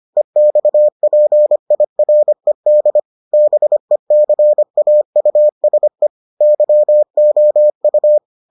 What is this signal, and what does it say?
Morse code, heard as a keyed tone, about 25 wpm, 600 Hz. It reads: EXPIRED BECAUSE YOU